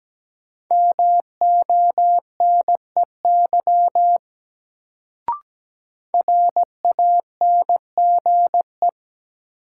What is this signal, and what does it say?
Morse code, heard as a keyed tone, about 17 wpm, 700 Hz. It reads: MONEY RANGE
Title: Sets of 4 Words - New Word Drill - 17wpm